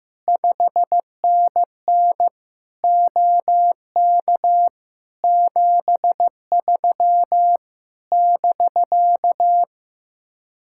Morse code, heard as a keyed tone, about 15 wpm, 700 Hz. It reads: 5NN OK 73 <BK>
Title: Speed-Racing Common QSO Phrases Encore - 15wpm